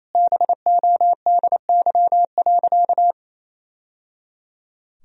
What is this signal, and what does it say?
Morse code, heard as a keyed tone, about 28 wpm, 700 Hz. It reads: BODY.
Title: Sentences from Top 1000 Words - Review Part 4 - 28wpm